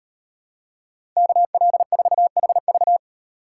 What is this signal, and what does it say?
Morse code, heard as a keyed tone, about 38 wpm, 700 Hz. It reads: KL4HV